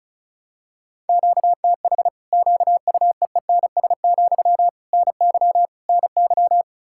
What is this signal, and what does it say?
Morse code, heard as a keyed tone, about 36 wpm, 700 Hz. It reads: QTH QUEENS, NY NY
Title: Practice QSO - Exchange 1 - Beginning - 35wpm